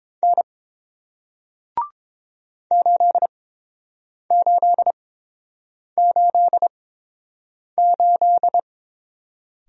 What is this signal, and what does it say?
Morse code, heard as a keyed tone, about 30 wpm, spaced as for 7 wpm, 700 Hz. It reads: N 8888